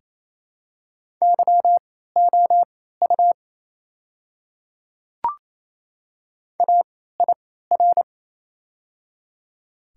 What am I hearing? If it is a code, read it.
Morse code, heard as a keyed tone, about 28 wpm, spaced as for 16 wpm, 700 Hz. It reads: YOU AIR
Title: 3x Character Spacing - Top 200 Words - 28wpm